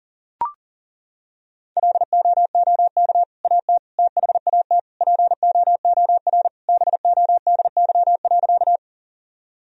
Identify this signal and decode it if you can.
Morse code, heard as a keyed tone, about 40 wpm, 700 Hz. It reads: LOOK AT THAT POOR BODY.
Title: Sentences from Top 1000 Words - Review Part 2 - 40wpm